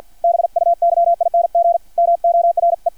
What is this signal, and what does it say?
Morse code, heard as a keyed tone, about 50 wpm, 680 Hz. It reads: BUYING MORE
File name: archived / CRSnet